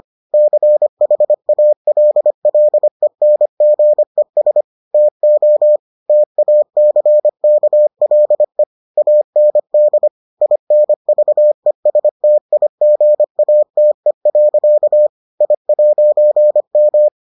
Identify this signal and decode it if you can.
Morse code, heard as a keyed tone, about 25 wpm, 600 Hz. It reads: CHALLENGES TO TACKLE AND INVESTIGATE. I'M